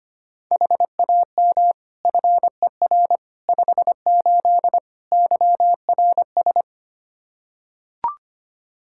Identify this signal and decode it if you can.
Morse code, heard as a keyed tone, about 25 wpm, 700 Hz. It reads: HAM FER 58 YRS
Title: Practice QSO - Exchange 3 - More Info - 25wpm